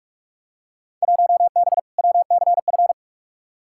Morse code, heard as a keyed tone, about 42 wpm, 700 Hz. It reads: 1B WCF
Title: Field Day Exchanges Encore - 45wpm